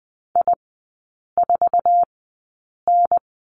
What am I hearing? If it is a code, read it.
Morse code, heard as a keyed tone, about 20 wpm, spaced as for 8 wpm, 700 Hz. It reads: I4N
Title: 047 Lesson 9e - Number 4 - Three Characters 20wpm